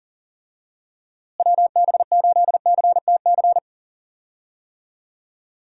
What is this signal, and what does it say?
Morse code, heard as a keyed tone, about 40 wpm, 700 Hz. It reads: WB8CTC
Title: USA Call Signs - 6 Characters - Repeated 3x - 40wpm